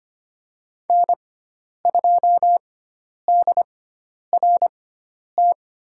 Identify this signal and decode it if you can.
Morse code, heard as a keyed tone, about 25 wpm, spaced as for 10 wpm, 700 Hz. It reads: N2DRT
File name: Lesson 16g - Letter c - Call Signs 25wpm